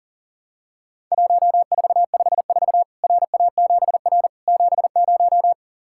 Morse code, heard as a keyed tone, about 37 wpm, 700 Hz. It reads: 1454 RA7R 70
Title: Worked All Europe DX Contest - QTC Exchange - 40wpm